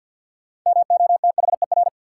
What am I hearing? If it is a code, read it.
Morse code, heard as a keyed tone, about 53 wpm, 700 Hz. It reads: MOTHER
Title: Sentences from Top 200 Words - New Word Drill - 50wpm